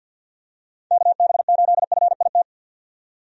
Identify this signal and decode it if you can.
Morse code, heard as a keyed tone, about 51 wpm, 700 Hz. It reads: KB8FIT
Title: USA Call Signs - 6 Characters - 50wpm